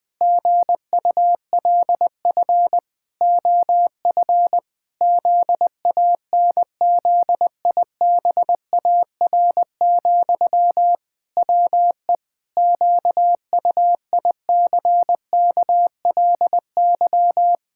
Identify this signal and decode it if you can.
Morse code, heard as a keyed tone, about 20 wpm, 700 Hz. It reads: GULF OF ZANZIBAR, WE QUICKLY
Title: Pangram Challenge v2 - 20wpm